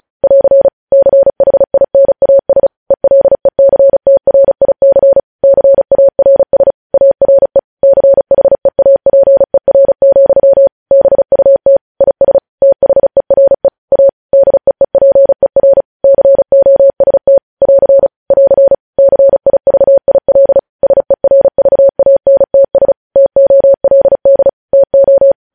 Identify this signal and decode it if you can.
Morse code, heard as a keyed tone, about 36 wpm, 550 Hz. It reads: <AR> CHINAS ELECTRIC CARS ARE CHEAPER, BUT IS THERE A DEEPER COST <AR> <AR> CIVIL SERVANTS TOLD TO
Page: news headlines in morse code at 35 wpm